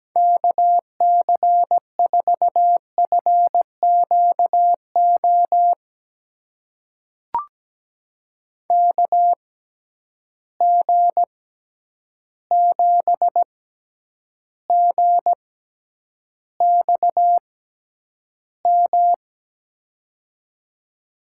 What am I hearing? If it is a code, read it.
Morse code, heard as a keyed tone, about 17 wpm, spaced as for 9 wpm, 700 Hz. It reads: KC4FQO KG7GXM
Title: USA Call Signs - 6 Characters - 6x Character Spacing - 17wpm